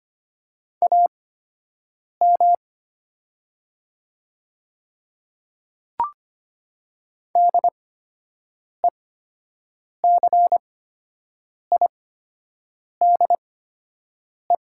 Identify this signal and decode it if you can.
Morse code, heard as a keyed tone, about 25 wpm, spaced as for 7 wpm, 700 Hz. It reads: AM DECIDE
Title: 8x Character Spacing - Top 500 Words - 25wpm